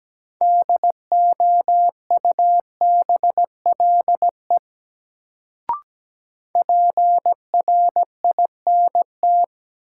Morse code, heard as a keyed tone, about 17 wpm, 700 Hz. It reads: DOUBLE PRINT
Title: Sentences from Top 1400 Words - New Word Drill Alt - 17wpm